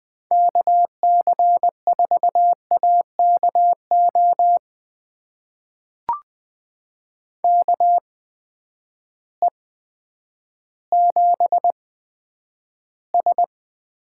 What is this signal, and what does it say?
Morse code, heard as a keyed tone, about 20 wpm, spaced as for 10 wpm, 700 Hz. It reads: KC4AKO KE7S
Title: USA Call Signs - 6 Characters - 8x Character Spacing - 20wpm